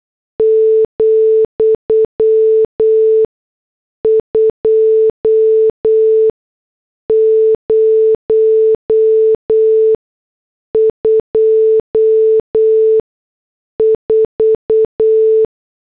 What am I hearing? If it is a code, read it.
Morse code, heard as a keyed tone, about 8 wpm, spaced as for 6 wpm, 440 Hz. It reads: , 2024